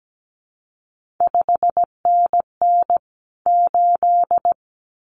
Morse code, heard as a keyed tone, about 17 wpm, 700 Hz. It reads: 5NN 8
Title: CQ WW DX Contest Exchange - 17wpm